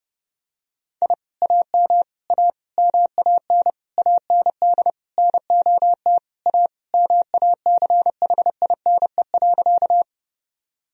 Morse code, heard as a keyed tone, about 30 wpm, 700 Hz. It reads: I AM A MAN AND NOT A MACHINE.
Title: Sentences from Top 1000 Words - Review Part 2 - 30wpm